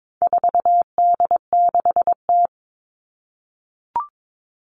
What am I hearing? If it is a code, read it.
Morse code, heard as a keyed tone, about 22 wpm, 700 Hz. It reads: 4D6T
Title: All Cut Numbers - 4-Digit 22wpm